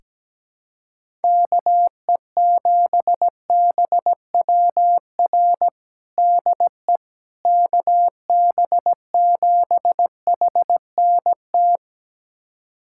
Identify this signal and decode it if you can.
Morse code, heard as a keyed tone, about 17 wpm, 700 Hz. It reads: KE7BWR DE KB7HNT
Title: USA Callsign DE Callsign - 6 Characters - 17wpm